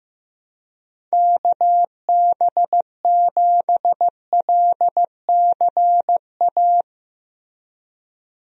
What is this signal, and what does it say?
Morse code, heard as a keyed tone, about 15 wpm, 700 Hz. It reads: KB7LCA
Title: USA Call Signs - 6 Characters - Repeated 2x - 15wpm